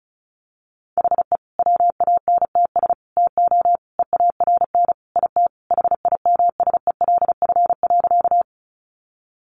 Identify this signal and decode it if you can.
Morse code, heard as a keyed tone, about 34 wpm, 700 Hz. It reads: HE WANTS TO EARN IT HIMSELF.
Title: Additional Easy Phrases 35wpm